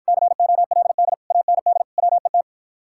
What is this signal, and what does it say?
Morse code, heard as a keyed tone, about 54 wpm, 700 Hz. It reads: COLD AND WET